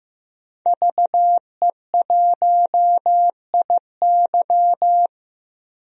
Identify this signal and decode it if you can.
Morse code, heard as a keyed tone, about 15 wpm, 700 Hz. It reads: VE1IY
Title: DXCC-100 Call Signs - 5 Characters - 15wpm